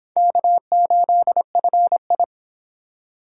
Morse code, heard as a keyed tone, about 26 wpm, 700 Hz. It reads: K8FI